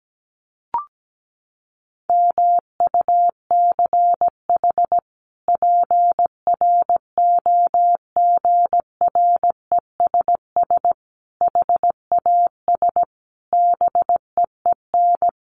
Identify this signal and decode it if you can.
Morse code, heard as a keyed tone, about 17 wpm, 700 Hz. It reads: MUCH PROGRESS HAS BEEN
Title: Sentences from Top 1400 Words 17wpm